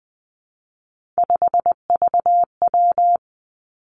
Morse code, heard as a keyed tone, about 20 wpm, 700 Hz. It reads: 5VW
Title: Three Challenging Letter-Number Combinations 20wpm